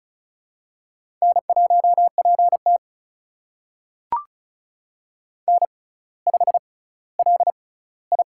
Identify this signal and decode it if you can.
Morse code, heard as a keyed tone, about 36 wpm, spaced as for 18 wpm, 700 Hz. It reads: N1PT N5LI
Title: USA Call Signs - 6x Character Spacing - 35wpm